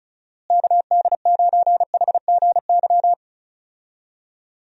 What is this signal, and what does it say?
Morse code, heard as a keyed tone, about 35 wpm, 700 Hz. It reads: KD9HGY
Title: USA Call Signs - 6 Characters - Repeated 2x - 35wpm